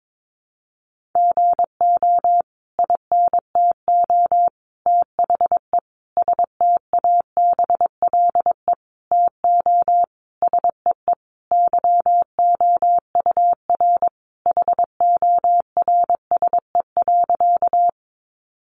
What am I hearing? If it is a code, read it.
Morse code, heard as a keyed tone, about 22 wpm, 700 Hz. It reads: GO INTO THE STABLE TO SEE YOUR HORSE.
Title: Sentences from Top 1500 Words 22wpm